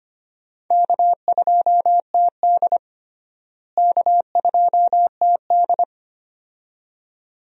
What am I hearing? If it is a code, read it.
Morse code, heard as a keyed tone, about 25 wpm, 700 Hz. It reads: K2TD K2TD
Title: USA Call Signs - 4 Characters - Repeated 2x - 25wpm